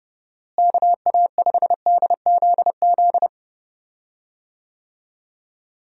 Morse code, heard as a keyed tone, about 32 wpm, 700 Hz. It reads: KA5DZZ
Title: USA Call Signs - 6 Characters - Repeated 2x - 30wpm